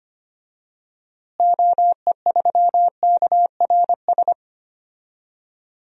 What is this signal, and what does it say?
Morse code, heard as a keyed tone, about 25 wpm, 700 Hz. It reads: OE3KRS
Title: DXCC-100 Call Signs - 6 Characters - Repeated 2x - 25wpm